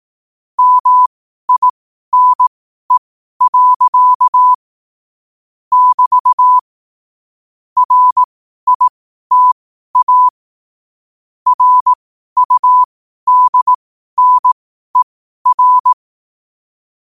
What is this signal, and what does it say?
Morse code, heard as a keyed tone, about 18 wpm, spaced as for 12 wpm, 1000 Hz. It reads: MINE. <BT> RITA RUDNER
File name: qotdfiles / 12wpm